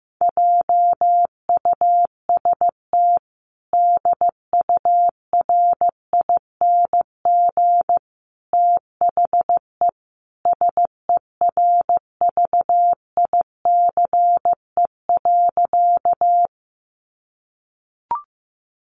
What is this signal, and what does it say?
Morse code, heard as a keyed tone, about 15 wpm, 700 Hz. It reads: JUST DURING THE SERVICE.
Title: Sentences from Top 700 Words 15wpm